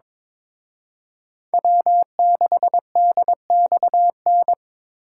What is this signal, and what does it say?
Morse code, heard as a keyed tone, about 26 wpm, 700 Hz. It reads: W6DXN
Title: Speed Racing - USA 1x3 Call Signs 22wpm